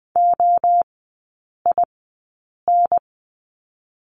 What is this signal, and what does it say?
Morse code, heard as a keyed tone, about 20 wpm, 700 Hz. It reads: O I N
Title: 029 Lesson 6e - Letter i - Three Characters 20wpm